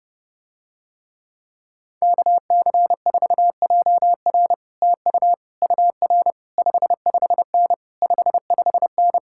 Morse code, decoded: KC4JR TU UR 55N 55N

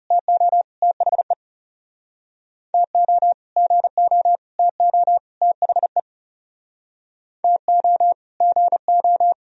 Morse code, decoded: TO THE TO GO TO THE TO GO

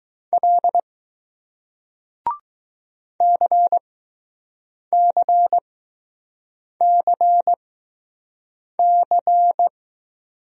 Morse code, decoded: L CCCC

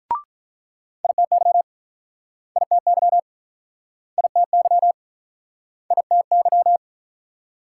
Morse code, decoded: ITY ITY ITY ITY

